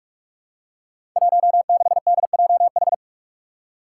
Morse code, heard as a keyed tone, about 44 wpm, 700 Hz. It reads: 16DJH
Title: Five Challenging Letter-Number Combinations 45wpm